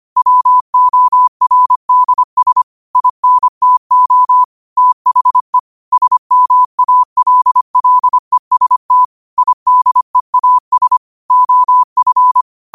Morse code, heard as a keyed tone, about 25 wpm, 1000 Hz. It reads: WORDS INTO THE SMALLEST IDEAS OF
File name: qotdfiles / 25wpm